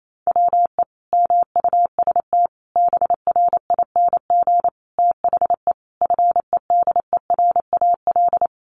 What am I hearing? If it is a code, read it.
Morse code, heard as a keyed tone, about 28 wpm, 700 Hz. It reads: WE MUST BRING THE FEDERAL
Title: Sentences from Top 1200 Words 28wpm